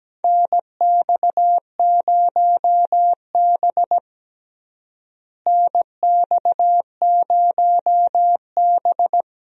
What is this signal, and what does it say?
Morse code, heard as a keyed tone, about 17 wpm, 700 Hz. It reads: NX0B NX0B
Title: USA Call Signs - 4 Characters - Repeated 3x - 17wpm